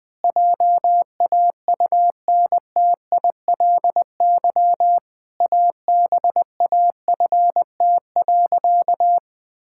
Morse code, heard as a keyed tone, about 20 wpm, 700 Hz. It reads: JAUNTILY ABAFT.
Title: Pangram Challenge v2 - 20wpm